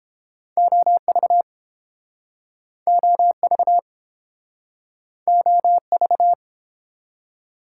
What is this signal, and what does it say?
Morse code, heard as a keyed tone, about 30 wpm, 700 Hz. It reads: OV OV OV